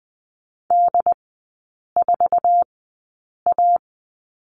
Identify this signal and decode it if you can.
Morse code, heard as a keyed tone, about 19 wpm, spaced as for 8 wpm, 700 Hz. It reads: D4A
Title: 127 Lesson 21e - Number 6 - Three Characters 20wpm